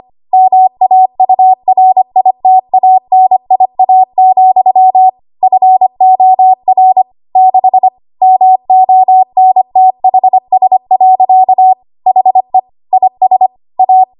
Morse code, heard as a keyed tone, about 25 wpm, 750 Hz. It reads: MAURITANIA, FOR 6 MONTHS. HE IS A
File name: Archive / 25 WPM / 25WPM